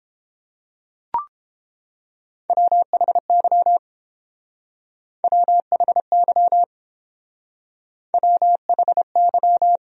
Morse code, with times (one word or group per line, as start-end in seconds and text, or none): 2.5-3.8 WHY
5.2-6.6 WHY
8.1-9.8 WHY